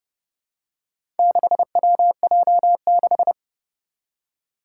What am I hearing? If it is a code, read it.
Morse code, heard as a keyed tone, about 28 wpm, 700 Hz. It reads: 6WJ6
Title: Four Challenging Letter-Number Combinations 30wpm